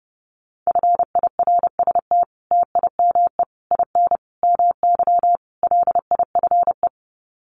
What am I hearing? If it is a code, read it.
Morse code, heard as a keyed tone, about 30 wpm, 700 Hz. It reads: FIRST TIME IN MY LIFE
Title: Sets of 5 Words Encore2 30wpm